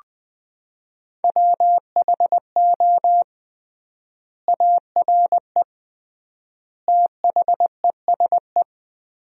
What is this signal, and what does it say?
Morse code, heard as a keyed tone, about 20 wpm, 700 Hz. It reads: WHO ARE THESE
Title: 3x Word-Spacing - Sentences from Top 200 Words - 20wpm